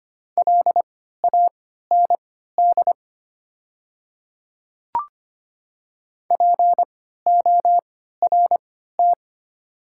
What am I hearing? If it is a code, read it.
Morse code, heard as a keyed tone, about 25 wpm, spaced as for 14 wpm, 700 Hz. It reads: LAND PORT
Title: 3x Character Spacing - Top 300 Words - 25wpm